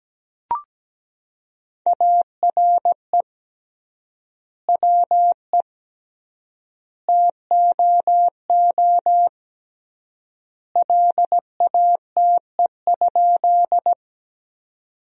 Morse code, heard as a keyed tone, about 17 wpm, 700 Hz. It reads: ARE WE TOO LATE?
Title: 3x Word-Spacing - Sentences from Top 300 Words - 17wpm